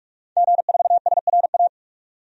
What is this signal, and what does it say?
Morse code, heard as a keyed tone, about 46 wpm, 700 Hz. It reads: G4SRA
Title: DXCC-100 Call Signs - 5 Characters - 45wpm